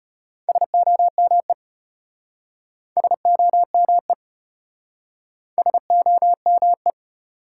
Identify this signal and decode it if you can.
Morse code, heard as a keyed tone, about 33 wpm, 700 Hz. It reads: SOME SOME SOME